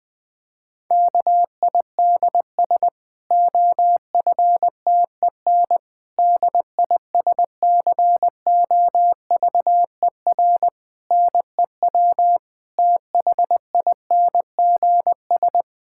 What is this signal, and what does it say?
Morse code, heard as a keyed tone, about 20 wpm, 700 Hz. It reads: KIDS OFTEN DISCOVER NEW THINGS
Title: Difficult Sentences with Hints - 20wpm